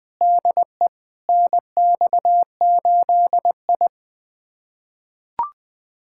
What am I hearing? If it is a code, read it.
Morse code, heard as a keyed tone, about 20 wpm, 700 Hz. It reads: DE NX8I
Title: USA Callsign DE Callsign - 4 Characters - 20wpm